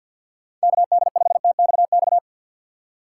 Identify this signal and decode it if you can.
Morse code, heard as a keyed tone, about 51 wpm, 700 Hz. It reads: KD5TXX
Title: K1USN SST QSO Practice - 50wpm